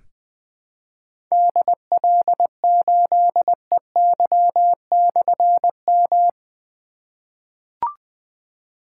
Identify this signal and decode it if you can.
Morse code, heard as a keyed tone, about 20 wpm, 700 Hz. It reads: DL8EY/M
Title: Difficult Call Signs 20wpm